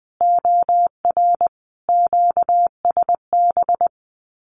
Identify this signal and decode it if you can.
Morse code, heard as a keyed tone, about 20 wpm, 700 Hz. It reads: OR QSB
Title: Common QSO Phrases 20wpm